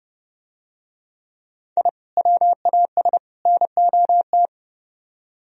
Morse code, heard as a keyed tone, about 30 wpm, 700 Hz. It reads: I WAS NOT